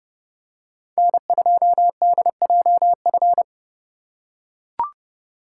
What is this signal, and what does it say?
Morse code, heard as a keyed tone, about 29 wpm, 700 Hz. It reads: N2DJF